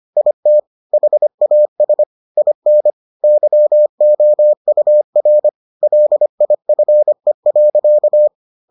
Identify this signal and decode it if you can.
Morse code, heard as a keyed tone, about 25 wpm, 600 Hz. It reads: IT HAS IN YOUR LIFE.